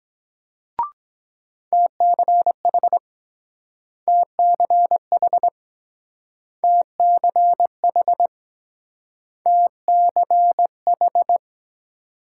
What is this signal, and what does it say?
Morse code, heard as a keyed tone, about 23 wpm, 700 Hz. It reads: TCH TCH TCH TCH